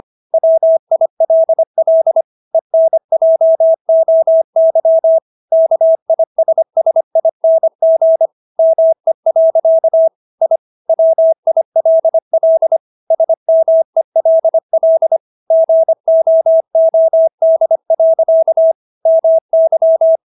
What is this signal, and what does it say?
Morse code, heard as a keyed tone, about 25 wpm, 650 Hz. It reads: WILL ENJOY KISSING ME. I WILL SMELL GOOD. MY